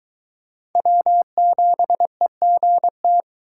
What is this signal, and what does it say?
Morse code, heard as a keyed tone, about 23 wpm, 700 Hz. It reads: W7EGT